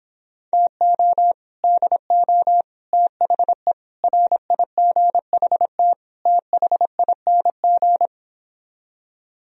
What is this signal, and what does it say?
Morse code, heard as a keyed tone, about 29 wpm, 700 Hz. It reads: TO DO THE RIGHT THING